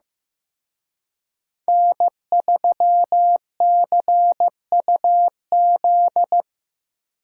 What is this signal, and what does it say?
Morse code, heard as a keyed tone, about 18 wpm, 700 Hz. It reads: N3CUZ
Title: Speed Racing - USA Call Signs 15wpm